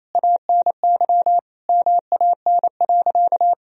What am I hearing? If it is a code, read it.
Morse code, heard as a keyed tone, about 28 wpm, 700 Hz. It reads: ANY MAN.
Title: Sentences from Top 1000 Words - Review Part 3 - 28wpm